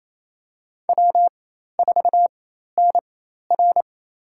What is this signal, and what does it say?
Morse code, heard as a keyed tone, about 28 wpm, spaced as for 13 wpm, 700 Hz. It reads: W4NR
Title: USA Call Signs - 4 Characters - 4x Character Spacing - 28wpm